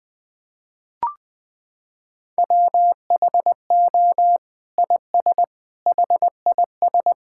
Morse code, heard as a keyed tone, about 20 wpm, 700 Hz. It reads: WHO IS HIS
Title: Sentences from Top 1600 Words 20wpm